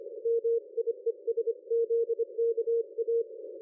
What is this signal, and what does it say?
Morse code, heard as a keyed tone, about 25 wpm, 460 Hz. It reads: MIESZKA
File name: dlazaawansowanych / nagraniazpasma / ExpertSDR2 line out 23.01.2020 14.12.00